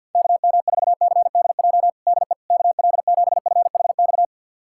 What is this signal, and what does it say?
Morse code, heard as a keyed tone, about 50 wpm, 700 Hz. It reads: KM4CDJ DE KL7FHX
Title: USA Callsign DE Callsign - 6 Characters - 50wpm